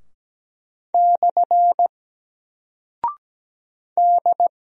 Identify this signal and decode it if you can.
Morse code, heard as a keyed tone, about 17 wpm, 700 Hz. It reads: / D
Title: Single Character (Letter:Number:Punctuation:Prosign) Alt 17wpm